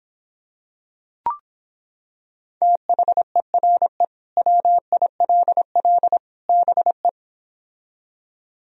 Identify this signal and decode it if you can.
Morse code, heard as a keyed tone, about 26 wpm, 700 Hz. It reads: THERE WILL BE